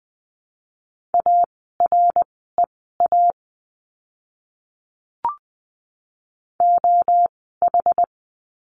Morse code, heard as a keyed tone, about 20 wpm, spaced as for 14 wpm, 700 Hz. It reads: AREA OH
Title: 061 Lesson 11f - Letter h - Words 20wpm